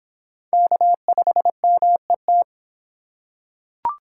Morse code, heard as a keyed tone, about 25 wpm, 700 Hz. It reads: K5MET